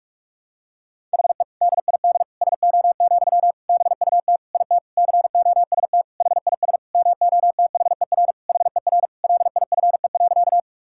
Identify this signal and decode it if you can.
Morse code, heard as a keyed tone, about 46 wpm, 700 Hz. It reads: HE DID SO, BUT IT COST HIS MOTHER HER LIFE.
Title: Sentences from Top 1000 Words - Review Part 5 - 45wpm